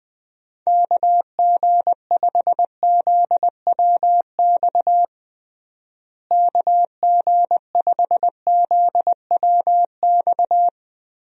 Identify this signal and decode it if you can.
Morse code, heard as a keyed tone, about 21 wpm, 700 Hz. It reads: KG5ZWX KG5ZWX
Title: USA Call Signs - 6 Characters - Repeated 2x - 20wpm